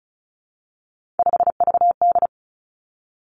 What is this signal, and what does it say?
Morse code, heard as a keyed tone, about 36 wpm, 700 Hz. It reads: 5VD
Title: Three Challenging Letter-Number Combinations 35wpm